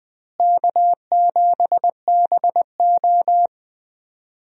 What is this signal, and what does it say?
Morse code, heard as a keyed tone, about 20 wpm, 700 Hz. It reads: K7BO
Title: USA Call Signs - 4 Characters - Repeated 3x - 20wpm